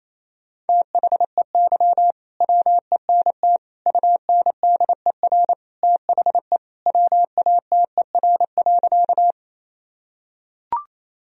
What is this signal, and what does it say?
Morse code, heard as a keyed tone, about 28 wpm, 700 Hz. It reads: THEY WENT UNDER THE WATER.
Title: Sentences from Top 1000 Words - Review Part 1 - 28wpm